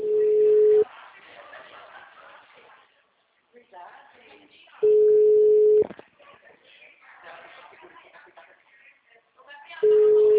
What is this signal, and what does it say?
Ringback tone (Europe, 425 Hz), with speech.